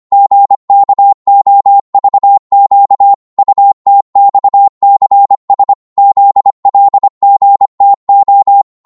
Morse code, heard as a keyed tone, about 25 wpm, 800 Hz. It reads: GKOVQ UTXCS ZLGTO
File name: Morse / ltr-25wpm